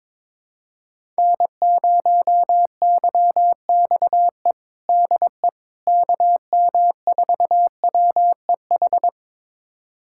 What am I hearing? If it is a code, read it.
Morse code, heard as a keyed tone, about 22 wpm, 700 Hz. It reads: N0YXE DE KM4WEH
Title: USA Callsign DE Callsign 22wpm